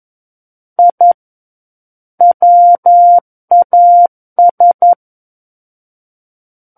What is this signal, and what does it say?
Morse code, heard as a keyed tone, about 11 wpm, 700 Hz. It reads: I WAS
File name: 11 WPM